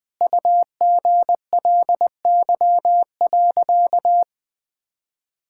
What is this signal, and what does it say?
Morse code, heard as a keyed tone, about 20 wpm, 700 Hz. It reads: UGLY.
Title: Sentences with 7-Word Phrases - 20wpm